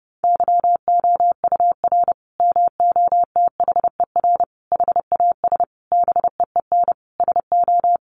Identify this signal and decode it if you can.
Morse code, heard as a keyed tone, about 30 wpm, 700 Hz. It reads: YOUR MOTHER HAS BEEN SO
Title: Sentences from Top 1200 Words 30wpm